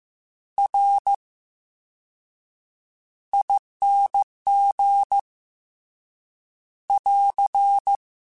Morse code, heard as a keyed tone, about 15 wpm, spaced as for 7 wpm, 800 Hz. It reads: RING <AR>
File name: Special INGs-mp3s / Special INGs-mp3s / VerySlowToFaster Ing @13